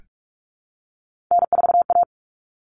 Morse code, heard as a keyed tone, about 44 wpm, 700 Hz. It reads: N4A
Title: Common Cut Numbers - 3-Digit 45wpm